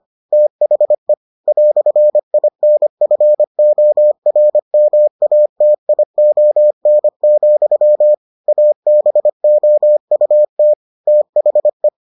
Morse code, heard as a keyed tone, about 25 wpm, 600 Hz. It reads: THE "INFORMATION, ABOUT THE